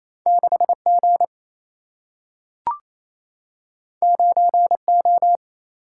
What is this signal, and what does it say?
Morse code, heard as a keyed tone, about 28 wpm, 700 Hz. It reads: 6 G 9 O
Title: Single-Letter Single-Number 28wpm